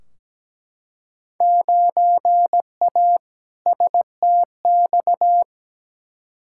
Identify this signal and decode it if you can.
Morse code, heard as a keyed tone, about 17 wpm, 700 Hz. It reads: 9A STX
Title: Field Day Exchanges - 17wpm